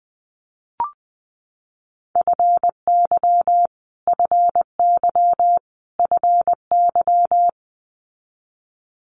Morse code, heard as a keyed tone, about 20 wpm, 700 Hz. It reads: FY FY FY
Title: Multi-letter Phonemes Repeated 3x 20wpm